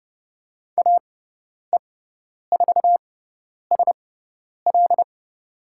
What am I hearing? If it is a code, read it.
Morse code, heard as a keyed tone, about 30 wpm, spaced as for 10 wpm, 700 Hz. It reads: AE4SL
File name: Lesson 17g - Letter u - Call Signs 30wpm